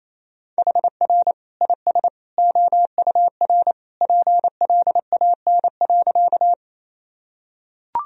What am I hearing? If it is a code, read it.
Morse code, heard as a keyed tone, about 28 wpm, 700 Hz. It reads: HR IS OUR PLAN.
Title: Sentences from Top 500 Words and Abbreviations 28wpm